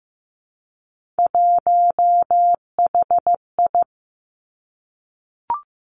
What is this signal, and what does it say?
Morse code, heard as a keyed tone, about 15 wpm, 700 Hz. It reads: 1HI
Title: Three Challenging Letter-Number Combinations 15wpm